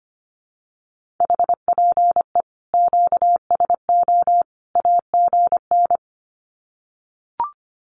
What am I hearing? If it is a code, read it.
Morse code, heard as a keyed tone, about 25 wpm, 700 Hz. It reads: HPE QSO AGN
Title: Common QSO Phrases 25wpm